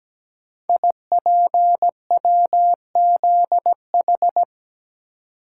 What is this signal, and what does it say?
Morse code, heard as a keyed tone, about 17 wpm, 700 Hz. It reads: IPWZH
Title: Code Groups - 5 Letters - 17wpm